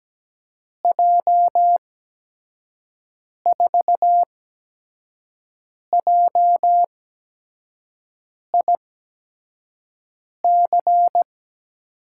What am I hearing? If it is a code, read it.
Morse code, heard as a keyed tone, about 17 wpm, spaced as for 5 wpm, 700 Hz. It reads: J4JIC
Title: USA Call Signs - 6 Characters - 8x Character Spacing - 17wpm